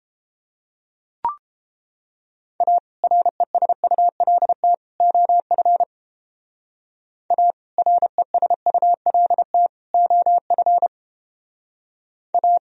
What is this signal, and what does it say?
Morse code, heard as a keyed tone, about 29 wpm, 700 Hz. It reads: A RESULT OF A RESULT OF A